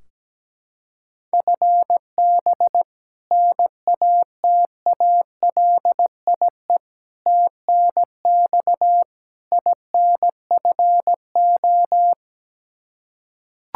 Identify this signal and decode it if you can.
Morse code, heard as a keyed tone, about 17 wpm, 700 Hz. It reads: FB NATALIE TNX INFO
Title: Practice QSO - Exchange 3 - More Info - 17wpm